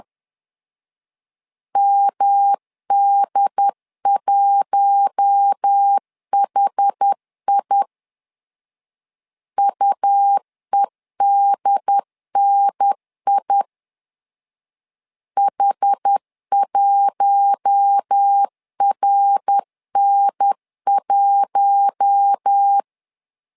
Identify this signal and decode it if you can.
Morse code, heard as a keyed tone, about 10 wpm, 800 Hz. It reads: MD1HI UEDNI H1RN1
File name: LEZIONI / 4 Q1 10wpm 2448